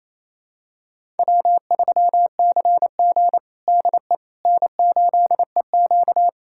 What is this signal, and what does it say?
Morse code, heard as a keyed tone, about 28 wpm, 700 Hz. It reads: W3CG DE N8EQ
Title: USA Callsign DE Callsign - 4 Characters - 28wpm